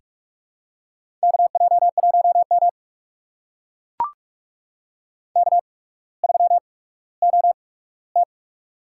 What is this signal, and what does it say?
Morse code, heard as a keyed tone, about 44 wpm, spaced as for 19 wpm, 700 Hz. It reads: KJ1M K3OT
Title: USA Call Signs - 4 Characters - 8x Character Spacing - 45wpm